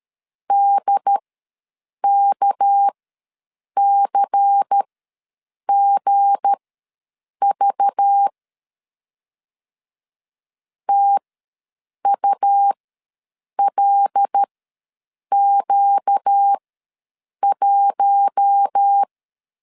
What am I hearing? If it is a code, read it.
Morse code, heard as a keyed tone, about 13 wpm, spaced as for 7 wpm, 800 Hz. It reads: DKCGV TULQ1
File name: LEZIONI / 7 GF9Z / 7 G ws 12wpm 3140